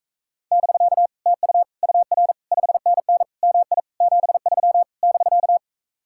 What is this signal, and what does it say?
Morse code, heard as a keyed tone, about 44 wpm, 700 Hz. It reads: <BK> TU UR 5NN MI 73 <BK>